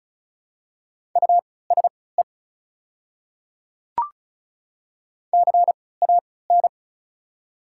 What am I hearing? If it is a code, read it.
Morse code, heard as a keyed tone, about 34 wpm, spaced as for 20 wpm, 700 Hz. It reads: USE CAN
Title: 3x Character Spacing - Top 500 Words - 35wpm